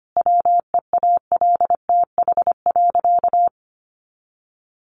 Morse code, heard as a keyed tone, about 25 wpm, 700 Hz. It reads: WEALTH.
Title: Sentences from Top 1200 Words 25wpm